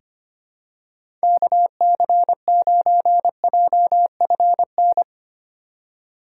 Morse code, heard as a keyed tone, about 25 wpm, 700 Hz. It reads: KC9JFN
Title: USA Call Signs - 6 Characters - 8x Character Spacing - 25wpm